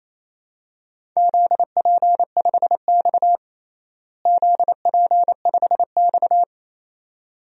Morse code, heard as a keyed tone, about 28 wpm, 700 Hz. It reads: ZP5X ZP5X
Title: DXCC-100 Call Signs - 4 Characters - Repeated 2x - 28wpm